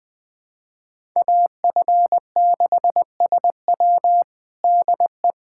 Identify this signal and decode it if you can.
Morse code, heard as a keyed tone, about 21 wpm, 700 Hz. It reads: AF6SW DE
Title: USA Callsign DE Callsign - 5 Characters - 20wpm